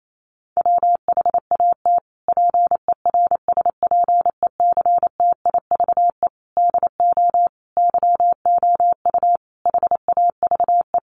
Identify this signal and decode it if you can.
Morse code, heard as a keyed tone, about 28 wpm, 700 Hz. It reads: WHAT PERSPECTIVE DO YOU HAVE
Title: Sentences from Top 1200 Words 28wpm